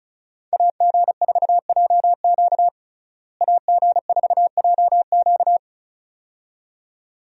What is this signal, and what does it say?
Morse code, heard as a keyed tone, about 34 wpm, 700 Hz. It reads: AG4JQ AG4JQ
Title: USA Call Signs - 5 Characters - Repeated 2x - 35wpm